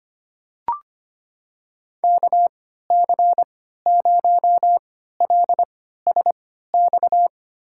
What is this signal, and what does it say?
Morse code, heard as a keyed tone, about 25 wpm, spaced as for 14 wpm, 700 Hz. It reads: KC0LSX
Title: USA Call Signs - 3x Character Spacing - 25wpm